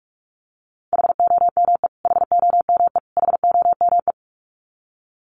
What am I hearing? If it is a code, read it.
Morse code, heard as a keyed tone, about 44 wpm, 700 Hz. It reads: HOME HOME HOME
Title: Top 500 Words Repeated 45wpm